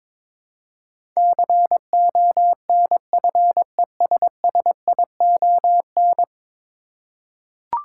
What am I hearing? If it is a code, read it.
Morse code, heard as a keyed tone, about 22 wpm, 700 Hz. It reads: CONFESSION
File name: practice / 22wpm / Ten Letter Words - 22wpm